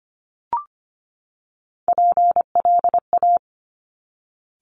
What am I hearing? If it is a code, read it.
Morse code, heard as a keyed tone, about 25 wpm, 700 Hz. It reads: PLA
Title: Top Three Letter Combinations in Words 25wpm